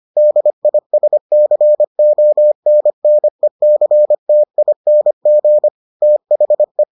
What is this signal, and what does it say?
Morse code, heard as a keyed tone, about 25 wpm, 600 Hz. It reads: DISCONNECTING THE